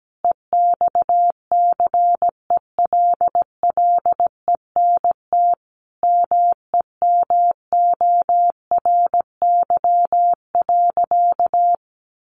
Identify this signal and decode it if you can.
Morse code, heard as a keyed tone, about 17 wpm, 700 Hz. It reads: EXCELLENT MEMORY.
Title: Sentences from Top 1300 Words 17wpm